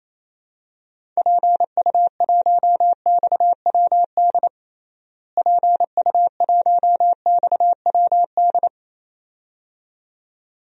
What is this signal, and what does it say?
Morse code, heard as a keyed tone, about 28 wpm, 700 Hz. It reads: PU1XWD PU1XWD
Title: DXCC-100 Call Signs - 6 Characters - Repeated 2x - 28wpm